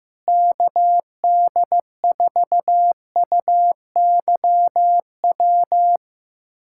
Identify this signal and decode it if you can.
Morse code, heard as a keyed tone, about 15 wpm, 700 Hz. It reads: KD4UYW
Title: USA Callsign DE Callsign - 6 Characters - 15wpm